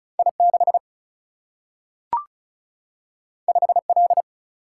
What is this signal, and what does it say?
Morse code, heard as a keyed tone, about 36 wpm, 700 Hz. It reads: I 6 5 L
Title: Single-Letter Single-Number 35wpm